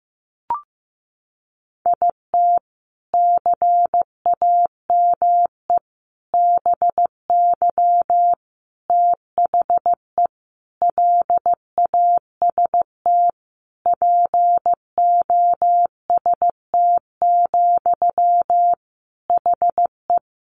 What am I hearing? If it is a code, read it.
Morse code, heard as a keyed tone, about 15 wpm, 700 Hz. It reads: IT CAME BY THE LAST POST, HE
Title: Sentences from Top 1000 Words 15wpm